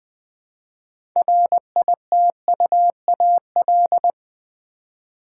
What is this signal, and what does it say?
Morse code, heard as a keyed tone, about 21 wpm, 700 Hz. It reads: RITUAL
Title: 6 Letter Words - First 3 Letters Given - 20wpm